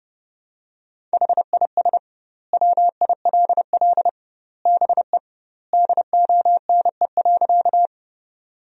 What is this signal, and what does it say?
Morse code, heard as a keyed tone, about 28 wpm, 700 Hz. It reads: HIS WILL BE DONE.
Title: 2x Word-Spacing - Sentences from Top 500 Words - 30wpm